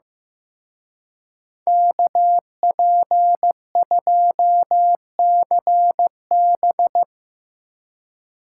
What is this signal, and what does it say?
Morse code, heard as a keyed tone, about 17 wpm, 700 Hz. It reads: KP2CB
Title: Speed Racing - USA 2x2 Call Signs 15wpm